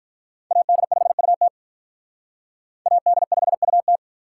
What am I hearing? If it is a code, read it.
Morse code, heard as a keyed tone, about 45 wpm, 700 Hz. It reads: AD5UT AD5UT